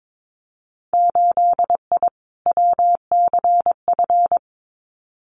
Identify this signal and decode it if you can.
Morse code, heard as a keyed tone, about 22 wpm, 700 Hz. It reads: 8I WCF
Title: Winter Field Day Exchanges Encore 22wpm